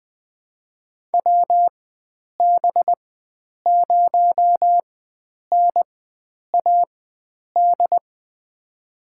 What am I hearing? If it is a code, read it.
Morse code, heard as a keyed tone, about 20 wpm, spaced as for 9 wpm, 700 Hz. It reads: WB0NAD
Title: USA Call Signs - 4x Character Spacing - 20wpm